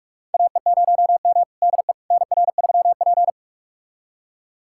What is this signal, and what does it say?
Morse code, heard as a keyed tone, about 46 wpm, 700 Hz. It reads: AE0M DE NR3P
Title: USA Callsign DE Callsign - 4 Characters - 45wpm